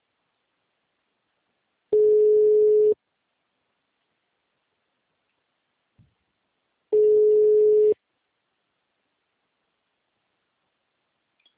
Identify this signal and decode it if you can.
Ringback tone (Europe, 425 Hz).